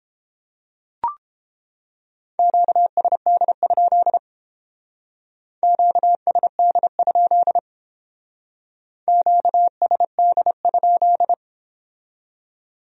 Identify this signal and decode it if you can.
Morse code, heard as a keyed tone, about 30 wpm, 700 Hz. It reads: QSD? QSD? QSD?